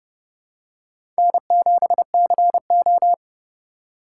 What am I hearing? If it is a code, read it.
Morse code, heard as a keyed tone, about 30 wpm, 700 Hz. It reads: N7CO
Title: USA Call Signs - 4 Characters - 2x Character Spacing - 30wpm